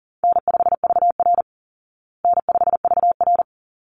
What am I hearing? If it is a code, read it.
Morse code, heard as a keyed tone, about 40 wpm, 700 Hz. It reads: N5VR N5VR
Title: Easy Call Signs Repeated 2x - 40wpm